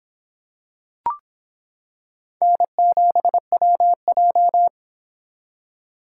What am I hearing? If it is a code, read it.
Morse code, heard as a keyed tone, about 25 wpm, 700 Hz. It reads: N7WJ